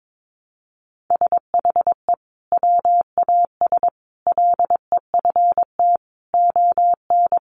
Morse code, heard as a keyed tone, about 22 wpm, 700 Hz. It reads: SHE WAS LEFT ON
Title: Sentences from Top 700 Words 22wpm